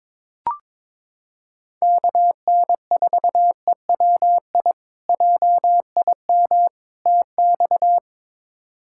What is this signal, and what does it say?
Morse code, heard as a keyed tone, about 22 wpm, 700 Hz. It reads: KN4EWI JIM TX
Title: K1USN SST QSO Practice - 22wpm